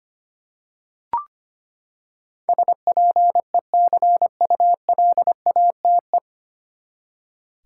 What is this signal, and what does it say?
Morse code, heard as a keyed tone, about 25 wpm, 700 Hz. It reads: SPECULATE
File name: Nine Letter Words - 25wpm